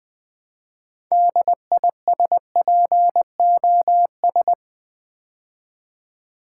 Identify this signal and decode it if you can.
Morse code, heard as a keyed tone, about 20 wpm, 700 Hz. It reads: DISPOS
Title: Build A Word - 3 Letters At A Time - 8-Letter Words - 20wpm